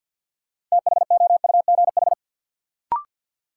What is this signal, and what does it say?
Morse code, heard as a keyed tone, about 47 wpm, 700 Hz. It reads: THOUGH
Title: Sentences from Top 400 Words - New Word Drill - 50wpm